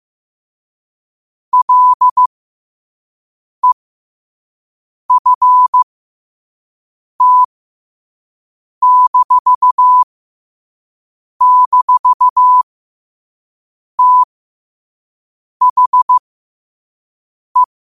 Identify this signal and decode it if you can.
Morse code, heard as a keyed tone, about 15 wpm, spaced as for 5 wpm, 1000 Hz. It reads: LEFT--THE
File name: qotdfiles / 5wpm